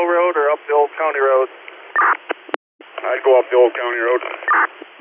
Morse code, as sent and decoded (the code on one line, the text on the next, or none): none